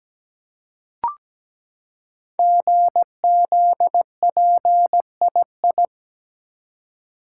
Morse code, heard as a keyed tone, about 17 wpm, 700 Hz. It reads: GZPII
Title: Code Groups - 5 Letter:Numbers - 17wpm